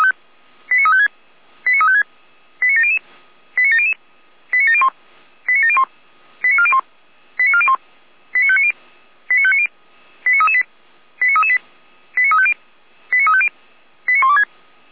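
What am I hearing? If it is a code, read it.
Five-tone selective call (ZVEI), heard as quick runs of five tones, tones of about 70 ms. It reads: 89356 89356 89700 89700 89701 89701 89401 89401 89590 89590 89338 89338 89350 89350 89126
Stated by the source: ZVEI five-tone selective call received by SDR.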